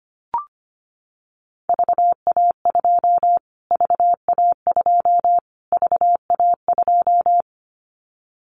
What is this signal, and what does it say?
Morse code, heard as a keyed tone, about 25 wpm, 700 Hz. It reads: VA2 VA2 VA2
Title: Callsign Prefixes for Top 25 DXCC - Repeated 3x - 25wpm